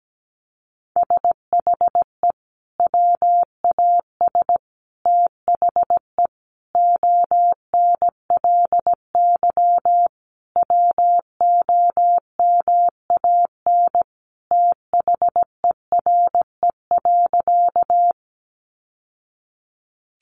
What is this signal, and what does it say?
Morse code, heard as a keyed tone, about 17 wpm, 700 Hz. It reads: SHE WAS THE ONLY WOMAN THERE.
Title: Sentences from Top 1100 Words 17wpm